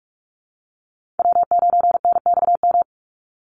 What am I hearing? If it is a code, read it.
Morse code, heard as a keyed tone, about 44 wpm, 700 Hz. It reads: W9NXM
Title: Call Signs Repeated 3x - 45wpm